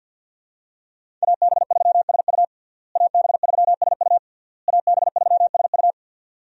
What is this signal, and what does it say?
Morse code, heard as a keyed tone, about 52 wpm, 700 Hz. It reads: AB3SU AB3SU AB3SU
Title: USA Call Signs - 5 Characters - Repeated 3x - 50wpm